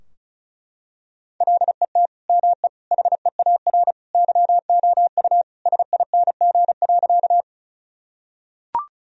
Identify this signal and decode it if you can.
Morse code, heard as a keyed tone, about 34 wpm, 700 Hz. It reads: LET ME HEAR YOU SING.
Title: Sentences from Top 1000 Words - Review Part 2 - 35wpm